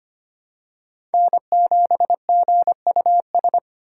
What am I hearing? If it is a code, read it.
Morse code, heard as a keyed tone, about 25 wpm, 700 Hz. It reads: N7GUS
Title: USA Call Signs - 5 Characters - 2x Character Spacing - 25wpm